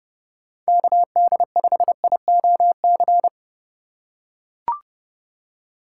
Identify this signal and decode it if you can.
Morse code, heard as a keyed tone, about 29 wpm, 700 Hz. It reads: KD5IOC